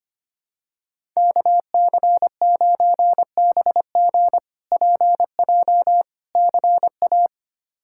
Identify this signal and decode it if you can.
Morse code, heard as a keyed tone, about 25 wpm, 700 Hz. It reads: KC9BG PJ CA
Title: K1USN SST QSO Practice - 25wpm